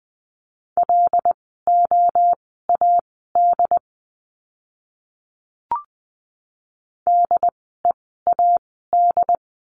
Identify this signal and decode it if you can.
Morse code, heard as a keyed tone, about 20 wpm, spaced as for 14 wpm, 700 Hz. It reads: LOAD DEAD
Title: 075 Lesson 13f - Letter L - Words 20wpm